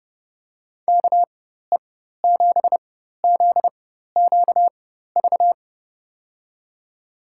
Morse code, heard as a keyed tone, about 30 wpm, spaced as for 14 wpm, 700 Hz. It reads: KE7ZQV
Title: USA Call Signs - 6 Characters - 4x Character Spacing - 30wpm